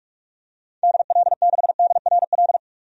Morse code, heard as a keyed tone, about 41 wpm, 700 Hz. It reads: DL6DRL
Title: DXCC-100 Call Signs - 6 Characters - 45wpm